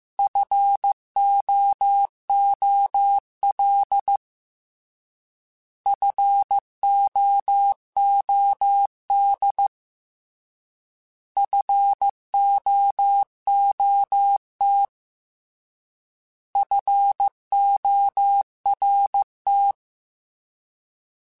Morse code, heard as a keyed tone, about 15 wpm, 800 Hz. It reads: FOOL FOOD FOOT FORT
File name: Alphabetical / Alphabetical mp3s / Fs @13